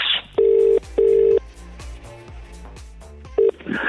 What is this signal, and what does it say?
Ringback tone (UK), 400 + 450 Hz.